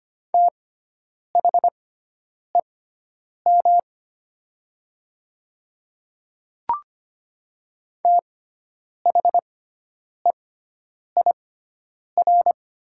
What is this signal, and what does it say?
Morse code, heard as a keyed tone, about 25 wpm, spaced as for 9 wpm, 700 Hz. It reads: THEM THEIR
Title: 6x Character Spacing - Top 300 Words - 25wpm